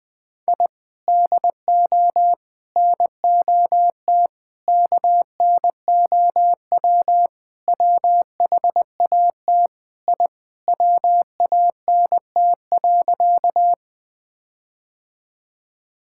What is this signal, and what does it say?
Morse code, heard as a keyed tone, about 20 wpm, 700 Hz. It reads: I DO NOT KNOW WHAT I WANT.
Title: Sentences with 5-Word Phrases - 20wpm